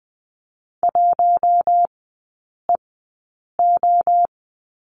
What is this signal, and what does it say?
Morse code, heard as a keyed tone, about 19 wpm, spaced as for 8 wpm, 700 Hz. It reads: 1EO
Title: 041 Lesson 8e - Number 1 - Three Characters 20wpm